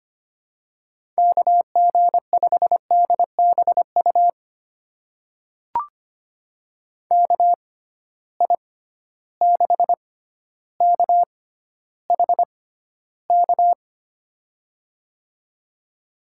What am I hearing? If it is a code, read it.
Morse code, heard as a keyed tone, about 25 wpm, spaced as for 13 wpm, 700 Hz. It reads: KG5DBU KI6KHK
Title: USA Call Signs - 6 Characters - 6x Character Spacing - 25wpm